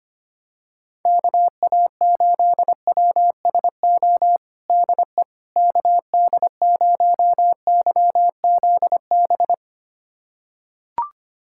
Morse code, heard as a keyed tone, about 25 wpm, 700 Hz. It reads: KA8WSO DE KD0YZB
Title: USA Callsign DE Callsign - 6 Characters - 25wpm